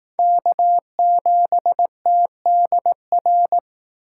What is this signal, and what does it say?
Morse code, heard as a keyed tone, about 18 wpm, 700 Hz. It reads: K7TDR
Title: Speed Racing - USA 1x3 Call Signs 15wpm